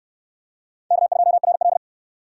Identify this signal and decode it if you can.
Morse code, heard as a keyed tone, about 68 wpm, 700 Hz. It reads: K2RL